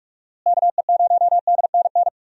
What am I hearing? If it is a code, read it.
Morse code, heard as a keyed tone, about 47 wpm, 700 Hz. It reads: KE0DNN
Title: USA Call Signs - 6 Characters - Repeated 2x - 45wpm